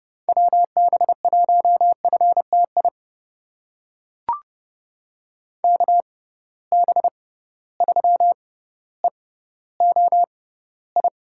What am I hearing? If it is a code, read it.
Morse code, heard as a keyed tone, about 32 wpm, spaced as for 16 wpm, 700 Hz. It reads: WB1FTI KB3EOI
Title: USA Call Signs - 6 Characters - 6x Character Spacing - 30wpm